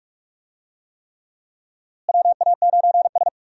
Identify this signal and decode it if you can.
Morse code, heard as a keyed tone, about 41 wpm, 700 Hz. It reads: WA9S
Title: USA Call Signs - 4 Characters - Repeated 3x - 45wpm